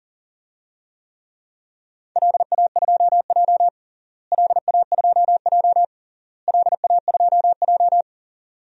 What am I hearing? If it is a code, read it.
Morse code, heard as a keyed tone, about 37 wpm, 700 Hz. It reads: LA2J LA2J LA2J
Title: DXCC-100 Call Signs - 4 Characters - Repeated 3x - 40wpm